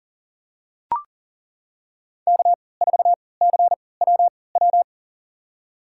Morse code, heard as a keyed tone, about 40 wpm, spaced as for 23 wpm, 700 Hz. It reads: K4CWW
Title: USA Call Signs - 5 Characters - 3x Character Spacing - 40wpm